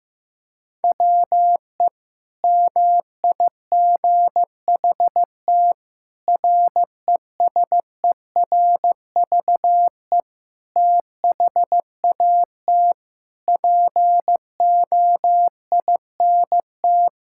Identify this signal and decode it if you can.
Morse code, heard as a keyed tone, about 14 wpm, 700 Hz. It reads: WE MIGHT RESERVE THAT POINT
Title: Sentences from Top 1000 Words - Review Part 6 - 15wpm